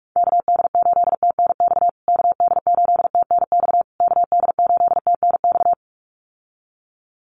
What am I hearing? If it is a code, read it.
Morse code, heard as a keyed tone, about 44 wpm, 690 Hz. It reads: KD8TNX KD8TNX KD8TNX
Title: USA Call Signs Repeated 3x 45wpm